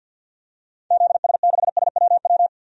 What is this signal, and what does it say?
Morse code, heard as a keyed tone, about 48 wpm, 700 Hz. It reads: ZS6SWW
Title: DXCC-100 Call Signs - 6 Characters - 50wpm